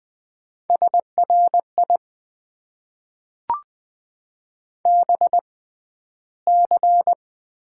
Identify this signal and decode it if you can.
Morse code, heard as a keyed tone, about 20 wpm, spaced as for 12 wpm, 700 Hz. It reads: SRI BC
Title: Top 90 QSO Elements - 6x Character Spacing - 20wpm